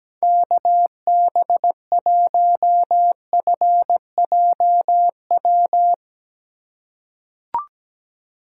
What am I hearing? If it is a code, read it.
Morse code, heard as a keyed tone, about 17 wpm, 700 Hz. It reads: KB1FJW
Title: USA Call Signs - 6 Characters - 2x Character Spacing - 17wpm